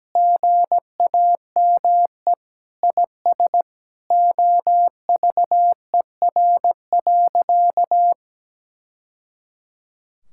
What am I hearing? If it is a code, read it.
Morse code, heard as a keyed tone, about 17 wpm, 700 Hz. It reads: GAME IS OVER.
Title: Sentences from Top 1000 Words - Review Part 6 - 17wpm